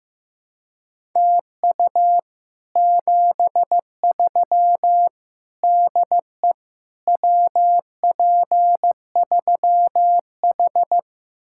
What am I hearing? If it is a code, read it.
Morse code, heard as a keyed tone, about 15 wpm, 700 Hz. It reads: TU 73 DE WP3H
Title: Speed-Racing Common QSO Phrases Encore - 15wpm